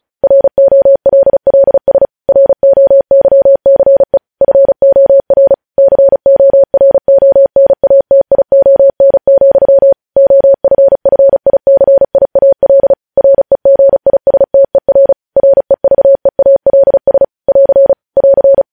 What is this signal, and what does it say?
Morse code, heard as a keyed tone, about 36 wpm, 550 Hz. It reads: ROLLS ROYCE FOR CORONATION, OFFICIAL REGISTER REVEALS <AR> <AR>
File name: podcast / News headlines-1x-35x35